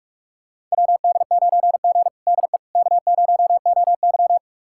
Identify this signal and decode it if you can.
Morse code, heard as a keyed tone, about 46 wpm, 700 Hz. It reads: WD9G DE K0OY
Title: USA Callsign DE Callsign - 4 Characters - 45wpm